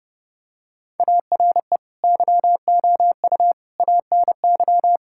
Morse code, heard as a keyed tone, about 30 wpm, 700 Hz. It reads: ARE YOU ANY
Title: Sentences from Top 1000 Words - Review Part 6 - 30wpm